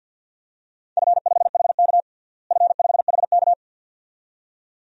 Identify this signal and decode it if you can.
Morse code, heard as a keyed tone, about 51 wpm, 700 Hz. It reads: F5HK F5HK
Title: DXCC-100 Call Signs - 4 Characters - Repeated 2x - 50wpm